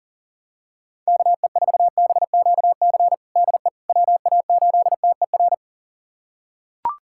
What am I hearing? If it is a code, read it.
Morse code, heard as a keyed tone, about 37 wpm, 700 Hz. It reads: KE4BQC DE WA8TER
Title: USA Callsign DE Callsign 40wpm